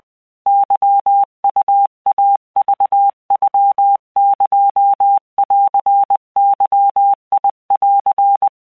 Morse code (Japanese Wabun code), heard as a keyed tone, about 20 wpm, 800 Hz. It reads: ケウイクノエンゲン
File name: keuikuchiyokugo20wptan